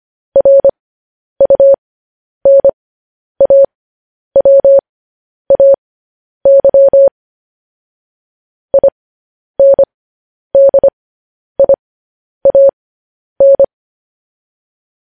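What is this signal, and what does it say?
Morse code, heard as a keyed tone, about 25 wpm, spaced as for 10 wpm, 550 Hz. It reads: RUNAWAY INDIAN
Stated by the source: Morse code news headlines practice, at 10 WPM.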